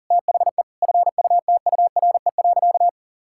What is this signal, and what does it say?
Morse code, heard as a keyed tone, about 40 wpm, 700 Hz. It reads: THE FUTURE.
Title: Sentences from Top 1000 Words - Review Part 6 - 40wpm